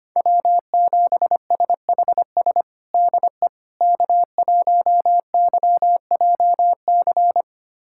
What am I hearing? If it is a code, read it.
Morse code, heard as a keyed tone, about 25 wpm, 700 Hz. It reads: W7SHS DE K1YJC
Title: USA Callsign DE Callsign - 5 Characters - 25wpm